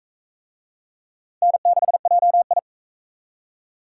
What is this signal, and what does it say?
Morse code, heard as a keyed tone, about 40 wpm, 700 Hz. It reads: N6JI